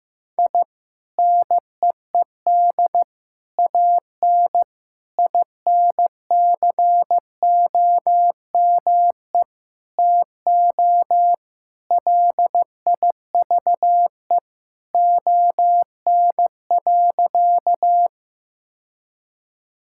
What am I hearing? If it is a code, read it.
Morse code, heard as a keyed tone, about 15 wpm, 700 Hz. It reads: I NEED AN INCOME TO LIVE ON.
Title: Sentences from Top 1000 Words - Review Part 5 - 15wpm